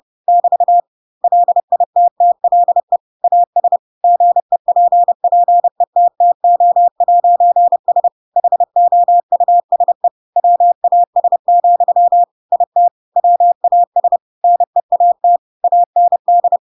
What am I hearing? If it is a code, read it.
Morse code, heard as a keyed tone, about 30 wpm, 700 Hz. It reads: <BT> LITTLE AS GEPPETTO'S HOUSE WAS, IT WAS NEAT AND
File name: Pinocchio / Pinocchio ch3 30 30 ik1hge